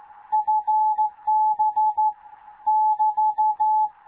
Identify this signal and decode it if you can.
Morse code, heard as a keyed tone, about 14 wpm, 860 Hz. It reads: FB <BT>
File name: Morze-HA2MN / hangp / 2 11 GYAKORLAT